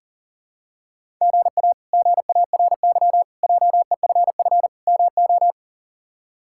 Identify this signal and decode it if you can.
Morse code, heard as a keyed tone, about 40 wpm, 700 Hz. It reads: GA GARY JEFF MO
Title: K1USN SST QSO Practice - 40wpm